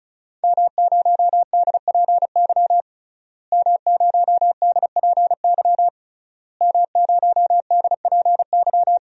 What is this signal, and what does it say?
Morse code, heard as a keyed tone, about 36 wpm, 700 Hz. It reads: M0DPY M0DPY M0DPY
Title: DXCC-100 Call Signs - 5 Characters - Repeated 3x - 35wpm